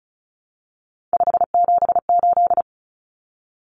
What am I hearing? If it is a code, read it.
Morse code, heard as a keyed tone, about 34 wpm, 700 Hz. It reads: 578
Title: Numbers - 3-Digit 35wpm